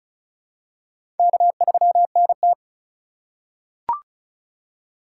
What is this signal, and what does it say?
Morse code, heard as a keyed tone, about 36 wpm, 700 Hz. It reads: K3NT
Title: USA Call Signs - 4 Characters - 2x Character Spacing - 35wpm